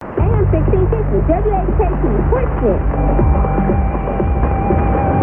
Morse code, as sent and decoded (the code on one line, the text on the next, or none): none